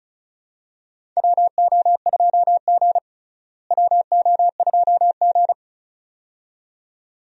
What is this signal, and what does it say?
Morse code, heard as a keyed tone, about 36 wpm, 700 Hz. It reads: WO2G WO2G
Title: USA Call Signs - 4 Characters - Repeated 2x - 35wpm